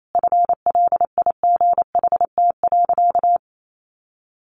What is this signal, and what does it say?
Morse code, heard as a keyed tone, about 28 wpm, 700 Hz. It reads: FLIGHT.
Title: Sentences from Top 1400 Words 28wpm